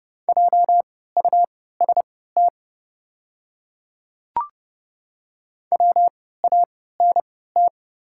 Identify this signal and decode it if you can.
Morse code, heard as a keyed tone, about 31 wpm, spaced as for 17 wpm, 690 Hz. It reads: JUST WANT
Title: 3x Character Spacing - Top 200 Words - 30wpm